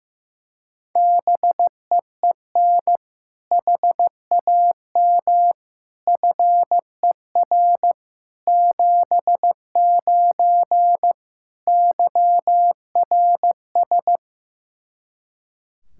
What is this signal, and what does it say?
Morse code, heard as a keyed tone, about 15 wpm, 700 Hz. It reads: BEEN HAM FER 79 YRS
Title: Practice QSO - Exchange 3 - More Info - 15wpm